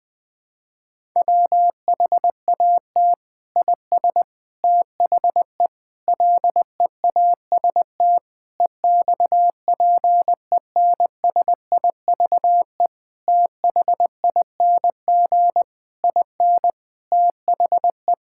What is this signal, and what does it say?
Morse code, heard as a keyed tone, about 20 wpm, 700 Hz. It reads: WHAT IS THE LEAST EXPENSIVE THING IN THE
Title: Sentences from Top 1000 Words - Review Part 6 - 20wpm